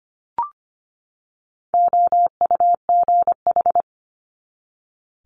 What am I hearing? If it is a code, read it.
Morse code, heard as a keyed tone, about 25 wpm, 700 Hz. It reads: OUGH
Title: Multi-letter Phonemes 25wpm